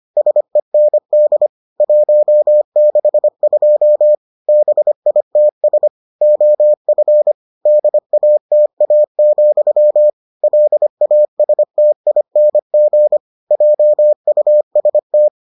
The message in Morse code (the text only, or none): SEND 162 BITS OF DATA, LASTING JUST